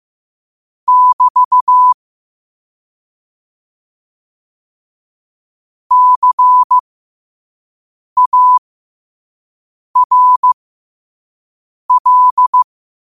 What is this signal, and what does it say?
Morse code, heard as a keyed tone, about 15 wpm, spaced as for 5 wpm, 1000 Hz. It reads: <BT> CARL
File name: qotdfiles / 5wpm